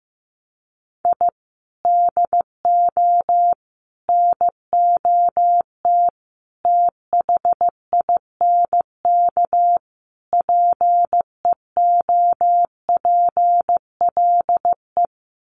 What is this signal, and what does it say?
Morse code, heard as a keyed tone, about 15 wpm, 700 Hz. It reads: I DO NOT THINK PEOPLE
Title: Sets of 5 Words Encore2 15wpm